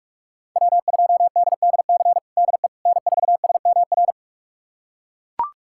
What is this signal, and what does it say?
Morse code, heard as a keyed tone, about 41 wpm, 700 Hz. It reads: W2DDC DE N4SMR
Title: USA Callsign DE Callsign - 5 Characters - 45wpm